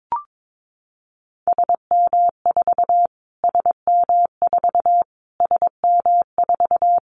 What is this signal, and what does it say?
Morse code, heard as a keyed tone, about 22 wpm, 700 Hz. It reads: SM4 SM4 SM4
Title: Callsign Prefixes for Top 25 DXCC - Repeated 3x - 22wpm